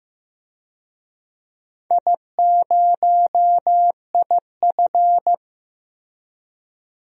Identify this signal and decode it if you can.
Morse code, heard as a keyed tone, about 15 wpm, 700 Hz. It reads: I0IF
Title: DXCC-100 Call Signs - 4 Characters - 15wpm